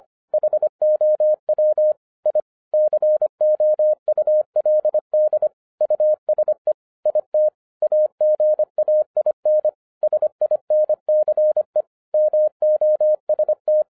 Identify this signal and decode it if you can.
Morse code, heard as a keyed tone, about 25 wpm, 610 Hz. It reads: HOW I COULD USE IT AGAIN SINCE MOST